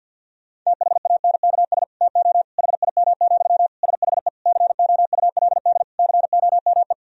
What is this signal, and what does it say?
Morse code, heard as a keyed tone, about 50 wpm, 700 Hz. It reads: THANKS TO HIM, SHE COULD COME
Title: Sentences from Top 1000 Words - Review Part 3 - 50wpm